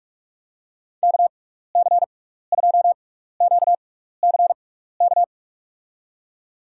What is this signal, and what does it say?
Morse code, heard as a keyed tone, about 46 wpm, spaced as for 16 wpm, 700 Hz. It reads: KC2QCK
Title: USA Call Signs - 6 Characters - 6x Character Spacing - 45wpm